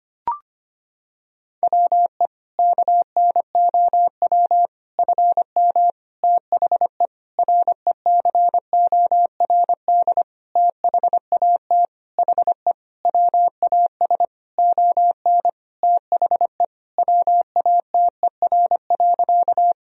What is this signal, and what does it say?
Morse code, heard as a keyed tone, about 25 wpm, 700 Hz. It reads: WE KNOW FM THE RECORD THAT HE WAS ON THE WATER.
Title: Sentences from Top 500 Words and Abbreviations 25wpm